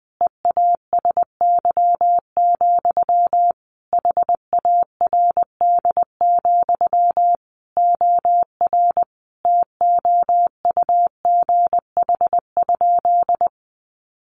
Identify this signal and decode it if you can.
Morse code, heard as a keyed tone, about 20 wpm, 700 Hz. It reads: EASY, HARD, OR TOUGH?
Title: Sentences from Top 1200 Words 20wpm